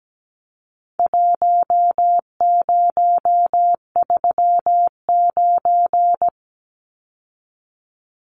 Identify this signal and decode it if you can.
Morse code, heard as a keyed tone, about 17 wpm, 700 Hz. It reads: 1039
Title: Numbers - Contest Style - 4-Digit 17wpm